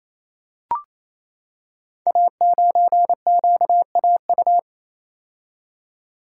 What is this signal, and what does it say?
Morse code, heard as a keyed tone, about 28 wpm, 700 Hz. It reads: A9QAU
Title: Code Groups - 5 Letter:Numbers - 28wpm